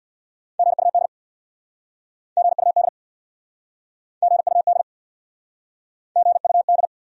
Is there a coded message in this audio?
Yes: Morse code from 0.6 to 6.9 s, heard as a keyed tone, about 69 wpm, 700 Hz. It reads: GUD GUD GUD GUD